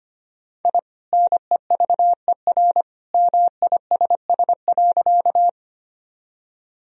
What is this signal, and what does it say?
Morse code, heard as a keyed tone, about 25 wpm, 700 Hz. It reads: I NEVER MISS.
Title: Sentences from Top 1000 Words - Review Part 3 - 25wpm